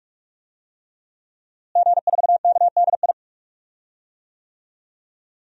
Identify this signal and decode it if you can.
Morse code, heard as a keyed tone, about 46 wpm, 700 Hz. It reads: G4KDI
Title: DXCC-100 Call Signs - 5 Characters - Repeated 3x - 45wpm